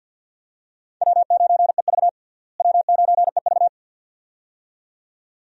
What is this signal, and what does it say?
Morse code, heard as a keyed tone, about 50 wpm, 700 Hz. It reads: W9EV W9EV
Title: USA Call Signs - 4 Characters - Repeated 2x - 50wpm